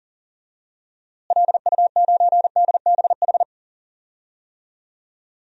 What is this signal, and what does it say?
Morse code, heard as a keyed tone, about 40 wpm, 700 Hz. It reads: LU9DBH
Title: DXCC-100 Call Signs - 6 Characters - Repeated 2x - 40wpm